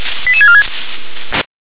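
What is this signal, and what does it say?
Five-tone selective call (ZVEI), heard as quick runs of five tones, tones of about 70 ms. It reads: 88646
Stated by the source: ZVEI five-tone call received with an SDR.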